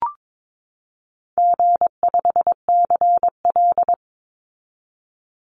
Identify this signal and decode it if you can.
Morse code, heard as a keyed tone, about 22 wpm, 700 Hz. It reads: G5CL
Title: Call Signs Repeated 2x - 22wpm